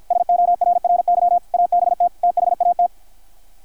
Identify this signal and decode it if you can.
Morse code, heard as a keyed tone, about 53 wpm, 720 Hz, with radio noise. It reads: SORRY ABT THAT
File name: archived / CRSnet